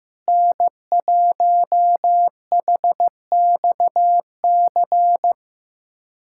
Morse code, heard as a keyed tone, about 15 wpm, 700 Hz. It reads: N1HXC
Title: USA Callsign DE Callsign - 5 Characters - 15wpm